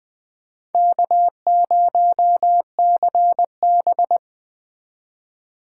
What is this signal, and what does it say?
Morse code, heard as a keyed tone, about 20 wpm, 700 Hz. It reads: K0CB